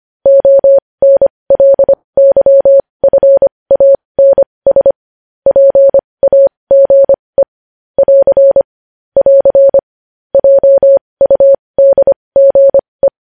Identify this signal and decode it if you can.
Morse code, heard as a keyed tone, about 25 wpm, spaced as for 20 wpm, 550 Hz. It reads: ONLYFANS PAGE <AR> <AR> JUDGE
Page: news-headlines-in-morse-code-at-20-wpm